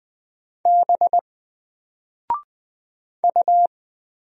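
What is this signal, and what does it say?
Morse code, heard as a keyed tone, about 20 wpm, 700 Hz. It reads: B U
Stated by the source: Single letter and number Morse drill at 20 wpm, from N4LCD.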